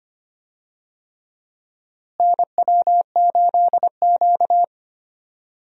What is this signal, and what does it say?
Morse code, heard as a keyed tone, about 25 wpm, 700 Hz. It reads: NW8Q
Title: USA Call Signs - 4 Characters - Repeated 3x - 25wpm